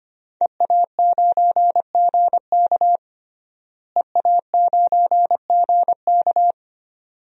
Morse code, heard as a keyed tone, about 25 wpm, 700 Hz. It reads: EA9GK EA9GK
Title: DXCC-100 Call Signs - 5 Characters - Repeated 2x - 25wpm